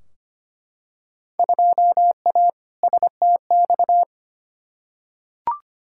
Morse code, heard as a keyed tone, about 25 wpm, 700 Hz. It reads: 2A STX
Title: Field Day Exchanges - 25wpm